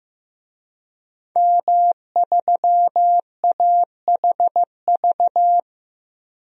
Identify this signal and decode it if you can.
Morse code, heard as a keyed tone, about 15 wpm, 700 Hz. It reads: M3AHV
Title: DXCC-100 Call Signs - 5 Characters - Repeated 2x - 15wpm